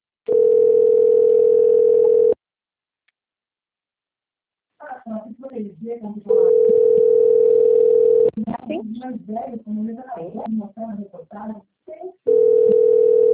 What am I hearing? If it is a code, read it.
Ringback tone (North America), 440 + 480 Hz, with speech.
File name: rg-1600-unknown- - - .26475